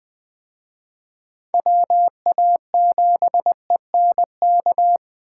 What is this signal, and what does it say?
Morse code, heard as a keyed tone, about 20 wpm, 700 Hz. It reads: WA7ENK